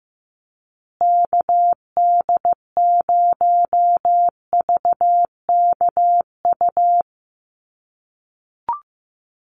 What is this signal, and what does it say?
Morse code, heard as a keyed tone, about 15 wpm, 700 Hz. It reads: KD0VKU
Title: USA Call Signs Repeated 2x 15wpm